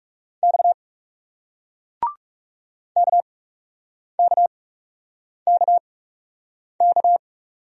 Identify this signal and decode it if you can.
Morse code, heard as a keyed tone, about 37 wpm, spaced as for 8 wpm, 700 Hz. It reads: X KKKK
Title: Speed-Racing Single Letter-Number 30wpm